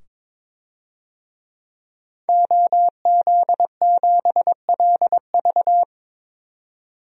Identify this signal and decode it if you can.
Morse code, heard as a keyed tone, about 22 wpm, 700 Hz. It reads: OZ7LV
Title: DXCC-100 Call Signs - 5 Characters - 22wpm